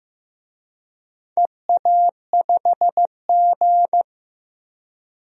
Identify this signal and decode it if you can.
Morse code, heard as a keyed tone, about 15 wpm, 700 Hz. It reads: EA5G
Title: DXCC-100 Call Signs - 4 Characters - Repeated 2x - 15wpm